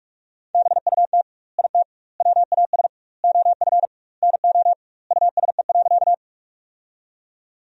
Morse code, heard as a keyed tone, about 45 wpm, 700 Hz. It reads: BUT IT WAS OF NO USE.
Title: 2x Word-Spacing - Sentences from Top 100 Words - 45wpm